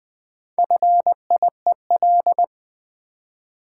Morse code, heard as a keyed tone, about 20 wpm, 700 Hz. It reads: FIEL
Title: Build A Word - 1 Letter At A Time - 3 to 5-Letter Words - 20wpm